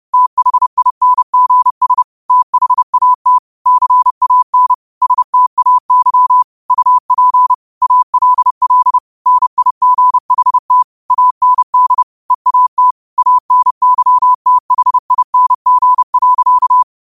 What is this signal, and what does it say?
Morse code, heard as a keyed tone, about 30 wpm, 1000 Hz. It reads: THINGS THAT CAN STAY UP ALL NIGHT AND EAT ANYTHING.
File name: qotdfiles / 30wpm